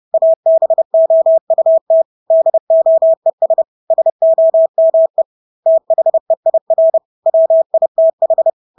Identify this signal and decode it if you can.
Morse code, heard as a keyed tone, about 30 wpm, 650 Hz. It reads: ABOUT DOES SOME THEIR WITH
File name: Engelske ord 30-30 wpm